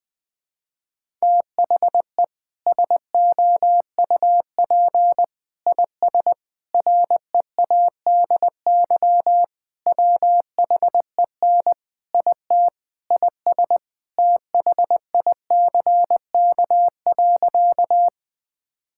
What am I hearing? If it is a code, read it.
Morse code, heard as a keyed tone, about 20 wpm, 700 Hz. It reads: THE SOUP IS READY WHEN IT IS THICK.
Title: Sentences from Top 1600 Words 20wpm